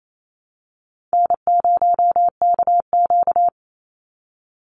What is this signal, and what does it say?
Morse code, heard as a keyed tone, about 28 wpm, 700 Hz. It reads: N0KQ
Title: Easy Call Signs - 28wpm